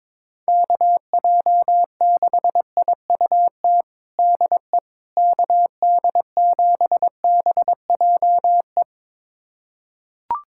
Morse code, heard as a keyed tone, about 22 wpm, 700 Hz. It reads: KJ6IUT DE KD7BJE
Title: USA Callsign DE Callsign 22wpm